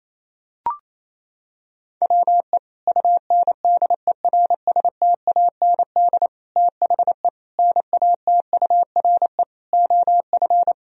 Morse code, heard as a keyed tone, about 28 wpm, 700 Hz. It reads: WE UNDERSTAND THE NATURE OF
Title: Sentences from Top 1000 Words - Review Part 2 - 28wpm